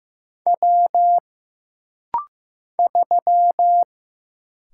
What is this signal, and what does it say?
Morse code, heard as a keyed tone, about 15 wpm, 700 Hz. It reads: W 3
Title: Single Letter-Number Alt - Rapid-Fire 15wpm